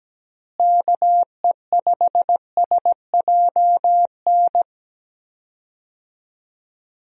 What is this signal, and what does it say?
Morse code, heard as a keyed tone, about 17 wpm, 700 Hz. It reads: KE5SJN
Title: USA Call Signs - 6 Characters - Repeated 3x - 17wpm